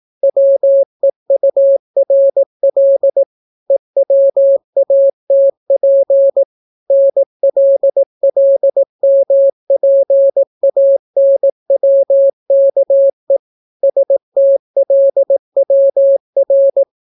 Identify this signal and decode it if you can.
Morse code, heard as a keyed tone, about 18 wpm, 550 Hz. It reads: WEURL EWATP NLLMPANWKE STLWR